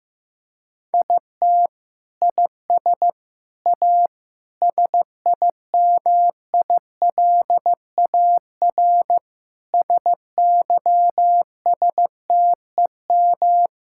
Morse code, heard as a keyed tone, about 15 wpm, 700 Hz. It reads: IT IS A SIMILAR SYSTEM
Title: Sentences from Top 1000 Words - Review Part 5 - 15wpm